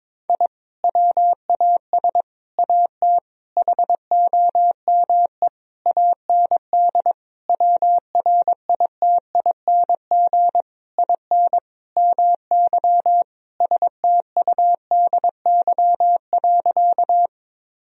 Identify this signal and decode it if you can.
Morse code, heard as a keyed tone, about 22 wpm, 700 Hz. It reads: I WAS AT HOME AND WRITING IN MY STUDY.
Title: Sentences from Top 1000 Words - Review Part 1 - 22wpm